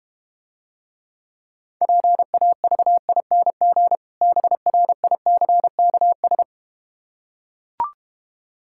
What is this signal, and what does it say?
Morse code, heard as a keyed tone, about 32 wpm, 690 Hz. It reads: PAVING BRICKS